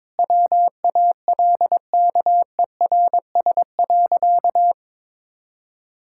Morse code, heard as a keyed tone, about 22 wpm, 700 Hz. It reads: WALKERS.
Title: Pangram Challenge v2 - 22wpm